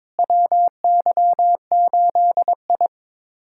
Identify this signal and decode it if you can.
Morse code, heard as a keyed tone, about 22 wpm, 700 Hz. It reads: WY8I
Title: USA Callsign DE Callsign - 4 Characters - 22wpm